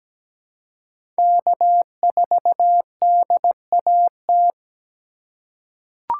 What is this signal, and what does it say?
Morse code, heard as a keyed tone, about 17 wpm, 700 Hz. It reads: K4DAT
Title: USA Call Signs - 5 Characters - 3x Character Spacing - 17wpm